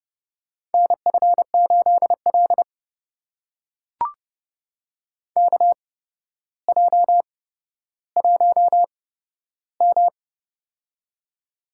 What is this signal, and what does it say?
Morse code, heard as a keyed tone, about 30 wpm, spaced as for 13 wpm, 700 Hz. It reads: NF8L KJ1M
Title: USA Call Signs - 4 Characters - 8x Character Spacing - 30wpm